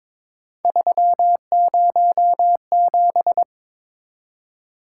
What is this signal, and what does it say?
Morse code, heard as a keyed tone, about 22 wpm, 700 Hz. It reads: 307
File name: practice / 22wpm / Three Letter-Number Combinations - 22wpm